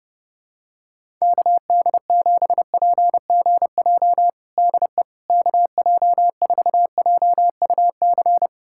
Morse code, heard as a keyed tone, about 32 wpm, 700 Hz. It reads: KD7PGJ DE KJ4JUC
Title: USA Callsign DE Callsign 30wpm